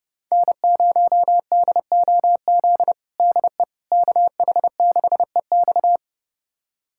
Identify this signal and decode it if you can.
Morse code, heard as a keyed tone, about 32 wpm, 700 Hz. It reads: N0DOZ DE KH6EX
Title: USA Callsign DE Callsign - 5 Characters - 30wpm